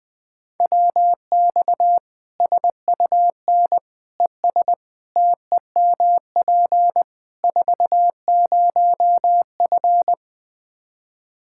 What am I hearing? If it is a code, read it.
Morse code, heard as a keyed tone, about 20 wpm, 700 Hz. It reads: WX SUN ES TEMP 40F
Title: Practice QSO - Exchange 2 - Info - 20wpm